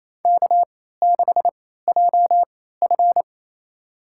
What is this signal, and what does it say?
Morse code, heard as a keyed tone, about 28 wpm, spaced as for 16 wpm, 700 Hz. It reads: K6JF
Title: USA Call Signs - 4 Characters - 3x Character Spacing - 28wpm